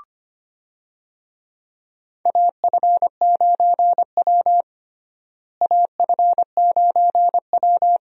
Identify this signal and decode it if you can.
Morse code, heard as a keyed tone, about 25 wpm, 700 Hz. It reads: AF9W AF9W
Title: USA Call Signs - 4 Characters - Repeated 2x - 25wpm